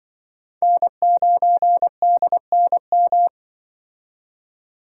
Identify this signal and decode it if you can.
Morse code, heard as a keyed tone, about 25 wpm, 700 Hz. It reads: N9DNM
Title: Speed Racing - USA 1x3 Call Signs 20wpm